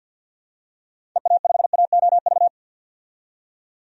Morse code, heard as a keyed tone, about 51 wpm, 700 Hz. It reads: EA5AOV
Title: DXCC-100 Call Signs - 6 Characters - 50wpm